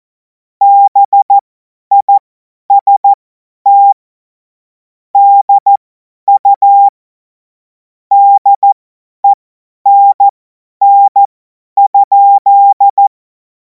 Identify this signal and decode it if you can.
Morse code, heard as a keyed tone, about 14 wpm, spaced as for 10 wpm, 800 Hz. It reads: BIST DU DENN?
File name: Morsekurs / Witze / Witz 21. April 2026 50 F70